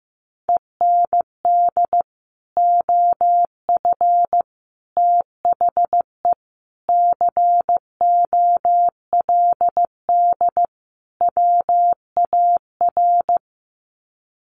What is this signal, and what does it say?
Morse code, heard as a keyed tone, about 15 wpm, 700 Hz. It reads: END OF THE COLD WAR
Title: Sets of 5 Words Encore 15wpm